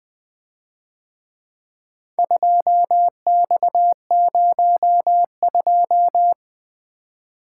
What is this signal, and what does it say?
Morse code, heard as a keyed tone, about 21 wpm, 700 Hz. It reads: 2X02
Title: Speed-Racing - Four Letter-Number Combinations - 20wpm